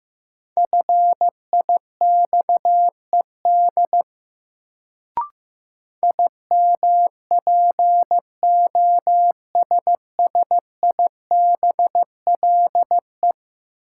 Morse code, heard as a keyed tone, about 15 wpm, 700 Hz. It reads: FIXED IMPOSSIBLE
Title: Sentences from Top 1100 Words - New Word Drill Alt - 15wpm